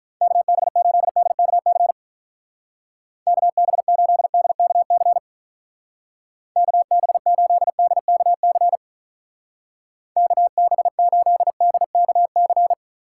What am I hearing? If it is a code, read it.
Morse code, heard as a keyed tone, about 42 wpm, 700 Hz. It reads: KB8DKC KB8DKC KB8DKC KB8DKC
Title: Speed Racing - USA 2x3 Call Signs 35wpm